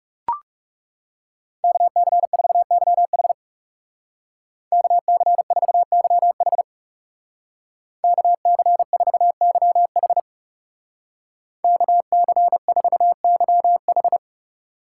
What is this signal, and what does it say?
Morse code, heard as a keyed tone, about 37 wpm, 700 Hz. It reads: KC4YH KC4YH KC4YH KC4YH
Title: Speed Racing - USA 2x2 Call Signs 30wpm